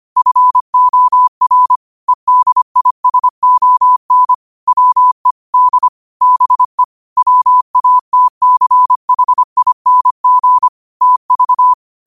Morse code, heard as a keyed tone, about 25 wpm, 1000 Hz. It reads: FOR EDISON WE D BE WATCHING TV
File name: qotdfiles / 25wpm